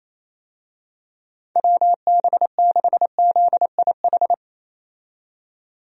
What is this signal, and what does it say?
Morse code, heard as a keyed tone, about 28 wpm, 700 Hz. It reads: WB6ZIH
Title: USA Call Signs - 6 Characters - 28wpm